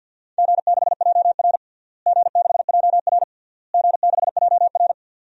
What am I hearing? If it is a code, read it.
Morse code, heard as a keyed tone, about 50 wpm, 700 Hz. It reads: G6JR G6JR G6JR
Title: DXCC-100 Call Signs - 4 Characters - Repeated 3x - 50wpm